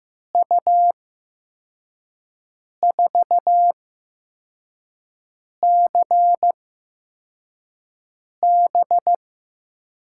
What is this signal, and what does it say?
Morse code, heard as a keyed tone, about 15 wpm, spaced as for 4 wpm, 700 Hz. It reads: U4CB
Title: USA Call Signs - 5 Characters - 8x Character Spacing - 15wpm